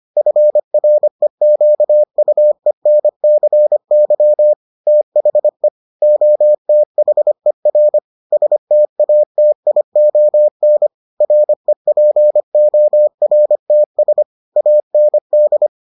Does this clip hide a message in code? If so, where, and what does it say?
Yes: Morse code from 0.2 to 15.7 s, heard as a keyed tone, about 25 wpm, 600 Hz. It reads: FREQUENCY THE OTHER STATION REPORTS AND